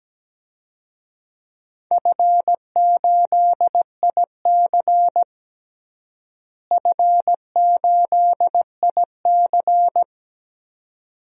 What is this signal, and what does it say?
Morse code, heard as a keyed tone, about 17 wpm, 700 Hz. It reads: F8IC F8IC
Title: DXCC-100 Call Signs - 4 Characters - Repeated 2x - 17wpm